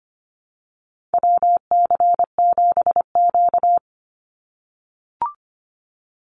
Morse code, heard as a keyed tone, about 25 wpm, 700 Hz. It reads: WC7Q
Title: Call Signs - 25wpm